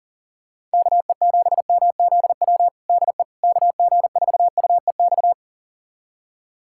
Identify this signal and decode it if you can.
Morse code, heard as a keyed tone, about 40 wpm, 700 Hz. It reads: KE7MZW DE KG4UEX
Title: USA Callsign DE Callsign - 6 Characters - 40wpm